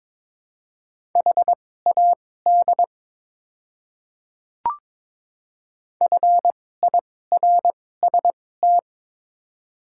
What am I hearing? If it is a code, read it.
Morse code, heard as a keyed tone, about 22 wpm, spaced as for 16 wpm, 700 Hz. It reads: HAD FIRST
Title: 2x Character Spacing - Top 300 Words - 22wpm